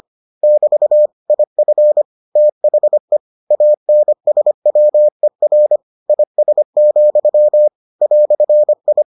Morse code, heard as a keyed tone, about 25 wpm, 600 Hz. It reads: <BT> IF THE ANSWER IS, "I